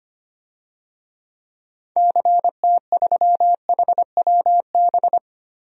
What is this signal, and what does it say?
Morse code, heard as a keyed tone, about 25 wpm, 700 Hz. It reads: CT3HWB
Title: DXCC-100 Call Signs - 6 Characters - Repeated 3x - 25wpm